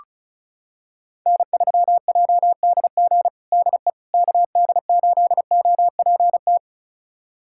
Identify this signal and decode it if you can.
Morse code, heard as a keyed tone, about 36 wpm, 700 Hz. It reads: N3JDG DE KD8OPT
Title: USA Callsign DE Callsign 35wpm